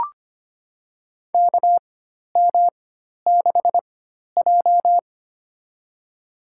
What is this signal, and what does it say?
Morse code, heard as a keyed tone, about 25 wpm, spaced as for 12 wpm, 700 Hz. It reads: KM6J
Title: USA Call Signs - 4x Character Spacing - 25wpm